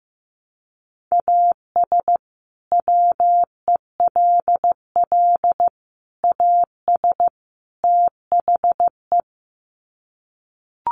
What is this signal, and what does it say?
Morse code, heard as a keyed tone, about 15 wpm, 700 Hz. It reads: AS WELL AS THE
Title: Sets of 4 Words 15wpm